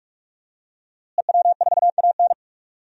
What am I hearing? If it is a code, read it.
Morse code, heard as a keyed tone, about 44 wpm, 700 Hz. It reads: EW4AN
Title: DXCC-100 Call Signs - 5 Characters - Repeated 3x - 45wpm